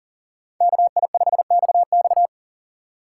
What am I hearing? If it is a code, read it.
Morse code, heard as a keyed tone, about 52 wpm, 700 Hz. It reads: KI5XX